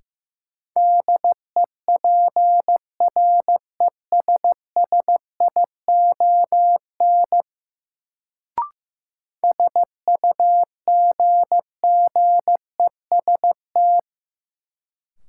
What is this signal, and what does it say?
Morse code, heard as a keyed tone, about 15 wpm, 700 Hz. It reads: DEPRESSION SUGGEST
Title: Sentences from Top 1300 Words - New Word Drill Alt - 15wpm